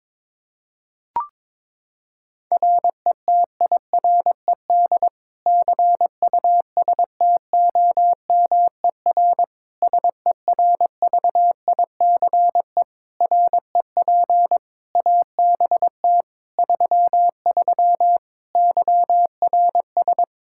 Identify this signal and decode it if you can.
Morse code, heard as a keyed tone, about 22 wpm, 700 Hz. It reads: RETIRED CUSTOMER SERVICE REP ABT 33 YRS
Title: Practice QSO - Exchange 3 - More Info - 22wpm